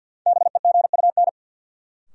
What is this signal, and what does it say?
Morse code, heard as a keyed tone, about 51 wpm, 700 Hz. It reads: BEGUN
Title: Sentences from Top 1000 Words - Review Part 6 - New Word Drill - 50wpm